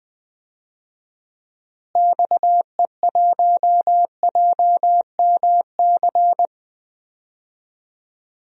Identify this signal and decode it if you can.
Morse code, heard as a keyed tone, about 21 wpm, 700 Hz. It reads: XE1JMC
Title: DXCC-100 Call Signs - 6 Characters - Repeated 2x - 20wpm